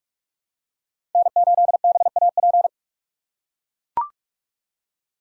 Morse code, heard as a keyed tone, about 40 wpm, 700 Hz. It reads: N8BAP